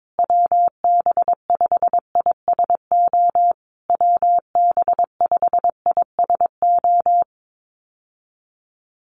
Morse code, heard as a keyed tone, about 22 wpm, 700 Hz. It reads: WB5ISO WB5ISO
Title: USA Call Signs Repeated 3x 22wpm